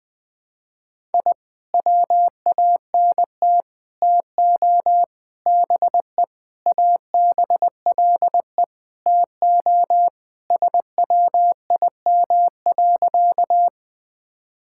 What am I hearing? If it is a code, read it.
Morse code, heard as a keyed tone, about 20 wpm, 700 Hz. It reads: I WANT TO BE ABLE TO SWIM.
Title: Sentences with 5-Word Phrases - 20wpm